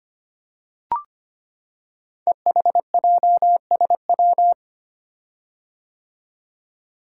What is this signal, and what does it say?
Morse code, heard as a keyed tone, about 25 wpm, 700 Hz. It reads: EHJSW
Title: Code Groups - 5 Letters - 25wpm